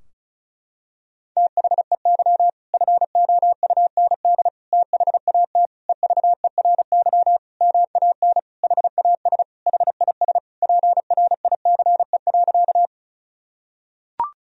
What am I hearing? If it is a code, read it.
Morse code, heard as a keyed tone, about 36 wpm, 700 Hz. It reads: THEY FOUND THAT EVERY MAN HAS HIS PRICE.
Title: Sentences from Top 1000 Words - Review Part 4 - 35wpm